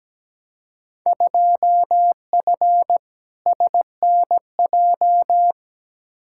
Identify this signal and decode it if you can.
Morse code, heard as a keyed tone, about 17 wpm, 700 Hz. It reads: 2F SNJ
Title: Field Day Exchanges Encore - 17wpm